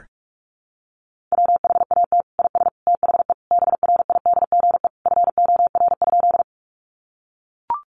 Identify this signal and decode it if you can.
Morse code, heard as a keyed tone, about 44 wpm, 700 Hz. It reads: WHAT IS THE BRIDGE FOR?
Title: Sentences from Top 1200 Words 45wpm